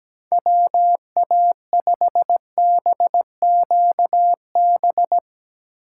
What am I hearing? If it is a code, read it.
Morse code, heard as a keyed tone, about 17 wpm, 700 Hz. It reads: WA5BQB
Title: USA Call Signs - 4x Character Spacing - 17wpm